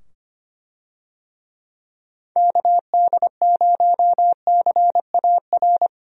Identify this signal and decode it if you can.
Morse code, heard as a keyed tone, about 25 wpm, 700 Hz. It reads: KD0CAR
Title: USA Call Signs - 6 Characters - 25wpm